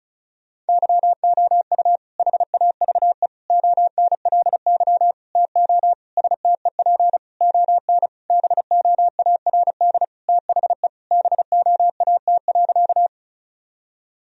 Morse code, heard as a keyed tone, about 36 wpm, 700 Hz. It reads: YOU HAVE ONLY TO STEP ON BOARD THE BOAT.
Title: Sentences from Top 1000 Words - Review Part 4 - 35wpm